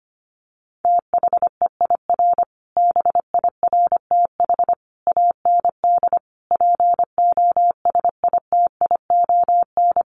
Morse code, heard as a keyed tone, about 25 wpm, 700 Hz. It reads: THEIR BIRTH AND POSITION
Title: Sentences from Top 900 Words 25wpm